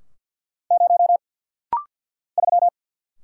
Morse code, heard as a keyed tone, about 50 wpm, 690 Hz. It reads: 0 3
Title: Single Letter-Number - Rapid-Fire 50wpm